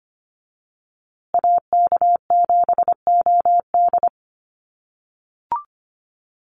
Morse code, heard as a keyed tone, about 25 wpm, 700 Hz. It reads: AK7OD
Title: USA Call Signs Repeated 3x 25wpm